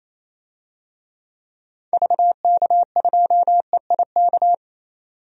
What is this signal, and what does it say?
Morse code, heard as a keyed tone, about 28 wpm, 700 Hz. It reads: VK2EIK
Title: DXCC-100 Call Signs - 6 Characters - 28wpm